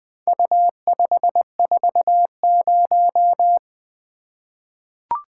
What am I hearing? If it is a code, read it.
Morse code, heard as a keyed tone, about 21 wpm, 700 Hz. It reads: U540
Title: Four Letter-Number Combinations - 20wpm